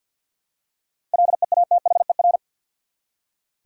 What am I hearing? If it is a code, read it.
Morse code, heard as a keyed tone, about 46 wpm, 700 Hz. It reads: LEATHER
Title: Seven Letter Words - 50wpm